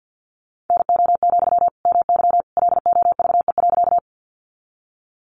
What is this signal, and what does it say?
Morse code, heard as a keyed tone, about 52 wpm, 700 Hz. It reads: NO, MY LOVE.
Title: Sentences from Top 400 Words 50wpm